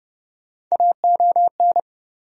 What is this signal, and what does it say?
Morse code, heard as a keyed tone, about 30 wpm, 700 Hz. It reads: AON
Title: USA Call Sign Postfixes - 3 Letters 30wpm